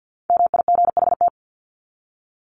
Morse code, heard as a keyed tone, about 50 wpm, 700 Hz. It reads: MIGHT
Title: Top 500 Words 50wpm